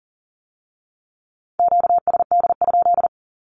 Morse code, heard as a keyed tone, about 40 wpm, 700 Hz. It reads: QSD?
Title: Q-Codes with Spoken Meaning - 40wpm